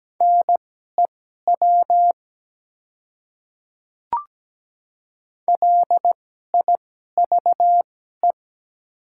Morse code, heard as a keyed tone, about 17 wpm, spaced as for 12 wpm, 700 Hz. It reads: NEW LIVE
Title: 2x Character Spacing - Top 200 Words - 17wpm